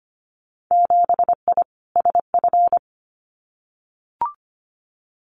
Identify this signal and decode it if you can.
Morse code, heard as a keyed tone, about 25 wpm, 700 Hz. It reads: 7I SF
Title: Winter Field Day Exchanges Encore 25wpm